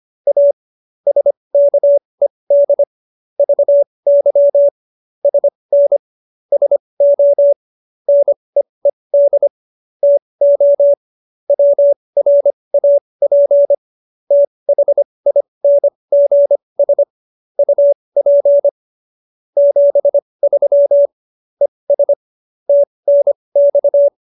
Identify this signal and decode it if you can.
Morse code, heard as a keyed tone, about 25 wpm, spaced as for 20 wpm, 570 Hz. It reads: A SKED VY SN SO NEED TO WRAP THINGS UP 73 ES TNX